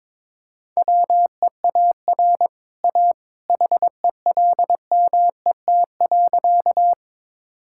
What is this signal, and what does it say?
Morse code, heard as a keyed tone, about 21 wpm, 700 Hz. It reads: WEAR A HELMET.
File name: practice / 22wpm / Easy Sentences with Hints - 22wpm